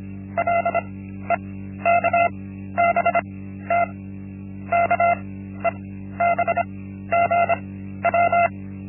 Morse code, heard as a keyed tone, about 25 wpm, spaced as for 13 wpm, 690 Hz. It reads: LEKBT KEBGW
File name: Linksmakalnis-1